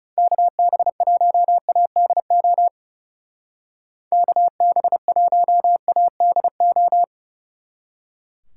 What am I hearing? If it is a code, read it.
Morse code, heard as a keyed tone, about 36 wpm, 700 Hz. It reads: KB1ADO KB1ADO
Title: Speed Racing - USA 2x3 Call Signs 30wpm